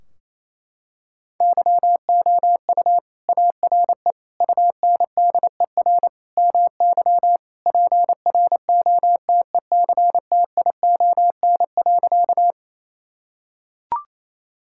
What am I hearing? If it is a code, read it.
Morse code, heard as a keyed tone, about 28 wpm, 700 Hz. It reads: YOU ARE UNDER MY PROTECTION.
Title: Sentences from Top 1600 Words 28wpm